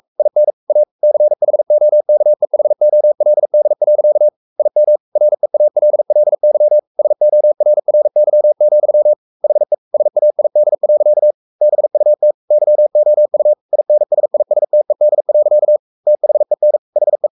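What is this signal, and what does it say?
Morse code, heard as a keyed tone, about 44 wpm, 600 Hz. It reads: IN A CHOKEHOLD. IM REALLY SORRY, HE SAID. BUT YOU INSISTED. THEN HE
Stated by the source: Morse code at 43 wpm sending Ken Follett's novel Never in English, from CW med Gnister.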